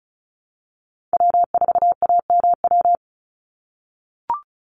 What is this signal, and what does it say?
Morse code, heard as a keyed tone, about 34 wpm, 700 Hz. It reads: W4AMW
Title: NA SOTA Chaser Callsigns 35wpm